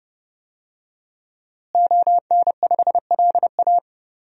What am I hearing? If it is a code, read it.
Morse code, heard as a keyed tone, about 31 wpm, 700 Hz. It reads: ON5LA
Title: DXCC-100 Call Signs - 5 Characters - Repeated 3x - 30wpm